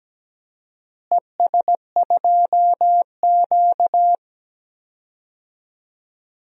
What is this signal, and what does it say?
Morse code, heard as a keyed tone, about 17 wpm, 700 Hz. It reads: ES2Q
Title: DXCC-100 Call Signs - 4 Characters - Repeated 2x - 17wpm